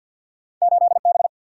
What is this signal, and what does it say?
Morse code, heard as a keyed tone, about 52 wpm, 700 Hz. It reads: 8 B
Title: Single-Letter Single-Number 50wpm